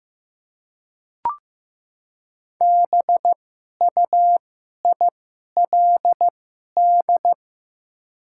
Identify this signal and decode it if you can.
Morse code, heard as a keyed tone, about 15 wpm, spaced as for 11 wpm, 700 Hz. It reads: BUILD
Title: 2x Character Spacing - Top 300 Words - 15wpm